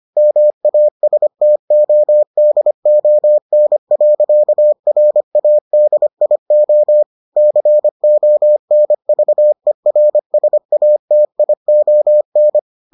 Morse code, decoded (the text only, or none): MASTODON.RADIO CONVERSATION